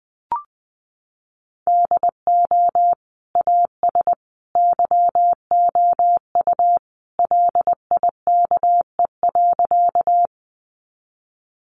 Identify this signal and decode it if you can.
Morse code, heard as a keyed tone, about 20 wpm, 700 Hz. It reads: DO AS YOU LIKE.
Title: Sentences from Top 100 Words 20wpm